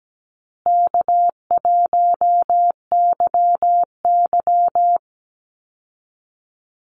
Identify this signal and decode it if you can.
Morse code, heard as a keyed tone, about 17 wpm, 700 Hz. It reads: K1YY
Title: Easy Call Signs Repeated 2x - 17wpm